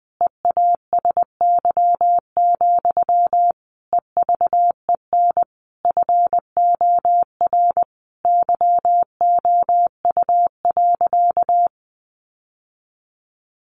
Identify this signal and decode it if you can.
Morse code, heard as a keyed tone, about 19 wpm, 700 Hz. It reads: EASY, EVEN FOR YOU.
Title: Sentences from Top 700 Words 20wpm